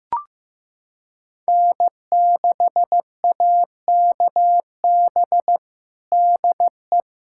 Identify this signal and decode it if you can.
Morse code, heard as a keyed tone, about 15 wpm, 700 Hz. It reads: N6AKB DE
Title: USA Callsign DE Callsign 15wpm